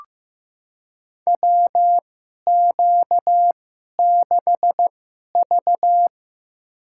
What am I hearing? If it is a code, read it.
Morse code, heard as a keyed tone, about 15 wpm, spaced as for 11 wpm, 700 Hz. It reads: WQ6V
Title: USA Call Signs - 4 Characters - 2x Character Spacing - 15wpm